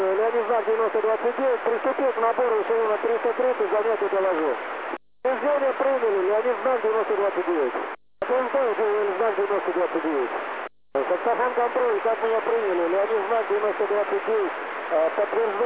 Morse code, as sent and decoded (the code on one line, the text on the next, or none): none